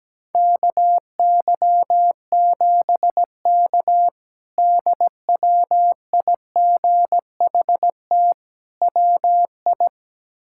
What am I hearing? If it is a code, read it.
Morse code, heard as a keyed tone, about 17 wpm, 700 Hz. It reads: KY7K DWIGHT WI
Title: K1USN SST QSO Practice - 17wpm